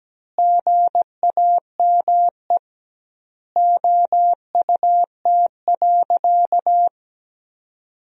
Morse code, heard as a keyed tone, about 17 wpm, 700 Hz. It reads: GAME OUT.
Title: 2x Word-Spacing - Sentences from Top 500 Words - 17wpm